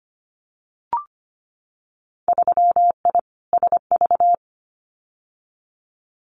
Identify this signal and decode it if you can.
Morse code, heard as a keyed tone, about 25 wpm, 700 Hz. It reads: 3I SV
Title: Winter Field Day Exchanges Encore 25wpm